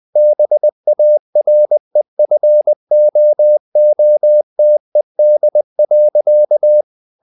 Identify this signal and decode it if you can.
Morse code, heard as a keyed tone, about 20 wpm, 600 Hz. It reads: BAREFOOTED.
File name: the old man and the sea / the old man and the sea part 1 20wpm